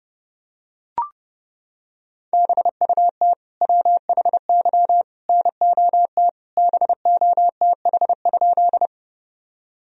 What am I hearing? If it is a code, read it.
Morse code, heard as a keyed tone, about 30 wpm, 700 Hz. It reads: BUT WHY NOT BOTH?
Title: Sentences from Top 1000 Words - Review Part 1 - 30wpm